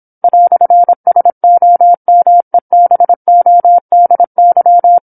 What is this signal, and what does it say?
Morse code, heard as a keyed tone, about 27 wpm, 700 Hz. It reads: "SOMEBODY
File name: 08-Chapter-08 26 WPM